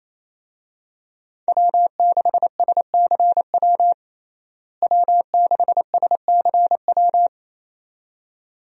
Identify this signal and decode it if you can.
Morse code, heard as a keyed tone, about 28 wpm, 700 Hz. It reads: W6SCW W6SCW
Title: USA Call Signs - 5 Characters - Repeated 2x - 28wpm